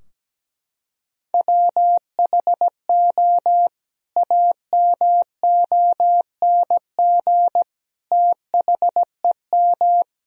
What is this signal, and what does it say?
Morse code, heard as a keyed tone, about 17 wpm, 700 Hz. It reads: WHO AMONG THEM
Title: Sentences from Top 1000 Words - Review Part 3 - 17wpm